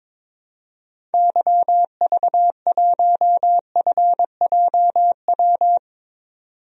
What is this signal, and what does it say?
Morse code, heard as a keyed tone, about 22 wpm, 700 Hz. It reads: YV1FJW
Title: DXCC-100 Call Signs - 6 Characters - 22wpm